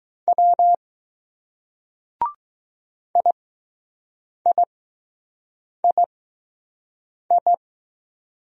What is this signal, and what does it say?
Morse code, heard as a keyed tone, about 20 wpm, spaced as for 6 wpm, 700 Hz. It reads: W IIII